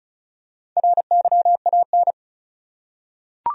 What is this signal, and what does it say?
Morse code, heard as a keyed tone, about 36 wpm, 700 Hz. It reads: RYAN
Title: USA Common Names - Max 5 Letters - 35wpm